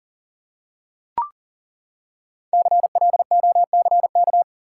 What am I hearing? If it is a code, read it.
Morse code, heard as a keyed tone, about 40 wpm, 700 Hz. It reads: CLOCK
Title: Five Letter Words - 40wpm